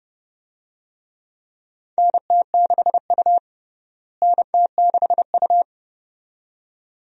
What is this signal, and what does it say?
Morse code, heard as a keyed tone, about 30 wpm, 700 Hz. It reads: NT6U NT6U
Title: USA Call Signs - 4 Characters - Repeated 2x - 30wpm